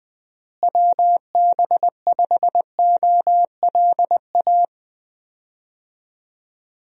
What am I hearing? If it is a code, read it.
Morse code, heard as a keyed tone, about 21 wpm, 700 Hz. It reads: WB5OLA
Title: USA Call Signs - 6 Characters - Repeated 2x - 20wpm